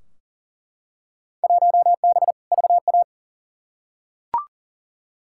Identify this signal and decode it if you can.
Morse code, heard as a keyed tone, about 39 wpm, 700 Hz. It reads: 1B VA
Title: Field Day Exchanges Encore - 40wpm